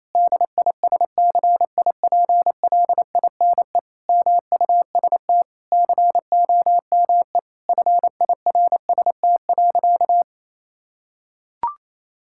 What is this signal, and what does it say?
Morse code, heard as a keyed tone, about 28 wpm, 700 Hz. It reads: DISCIPLINE MUST COME FIRST.
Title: Sentences from Top 1000 Words - Review Part 6 - 28wpm